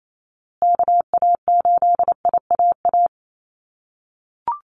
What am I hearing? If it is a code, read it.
Morse code, heard as a keyed tone, about 28 wpm, 700 Hz. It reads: KA8IAA
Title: USA Call Signs Repeated 3x 28wpm